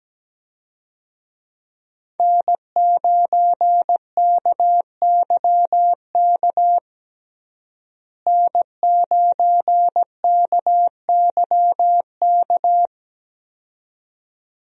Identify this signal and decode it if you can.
Morse code, heard as a keyed tone, about 17 wpm, 700 Hz. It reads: N9KYK N9KYK
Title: USA Call Signs - 5 Characters - Repeated 2x - 17wpm